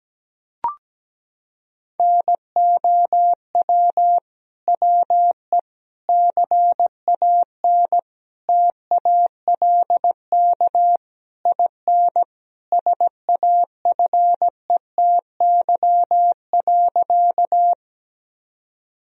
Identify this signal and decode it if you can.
Morse code, heard as a keyed tone, about 17 wpm, 700 Hz. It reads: NOW WE CAN TALK IN SAFETY.
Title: Sentences from Top 1000 Words - Review Part 4 - 17wpm